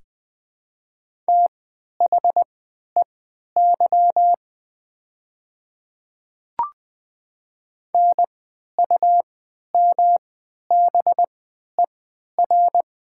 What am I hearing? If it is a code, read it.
Morse code, heard as a keyed tone, about 20 wpm, spaced as for 11 wpm, 700 Hz. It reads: THEY NUMBER
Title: 3x Character Spacing - Top 300 Words - 20wpm